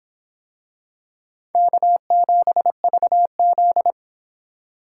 Morse code, heard as a keyed tone, about 24 wpm, 700 Hz. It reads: K7VZ